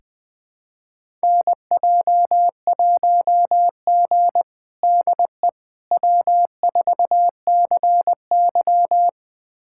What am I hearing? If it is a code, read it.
Morse code, heard as a keyed tone, about 21 wpm, 700 Hz. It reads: NJ1G DE W4CY
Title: USA Callsign DE Callsign - 4 Characters - 20wpm